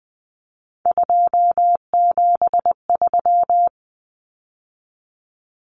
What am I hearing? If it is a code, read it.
Morse code, heard as a keyed tone, about 19 wpm, 700 Hz. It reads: 273
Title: Numbers - 3-Digit 20wpm